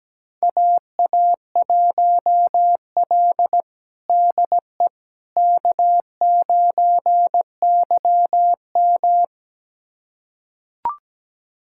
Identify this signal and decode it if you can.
Morse code, heard as a keyed tone, about 17 wpm, 700 Hz. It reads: AA1L DE K9YM
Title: USA Callsign DE Callsign - 4 Characters - 17wpm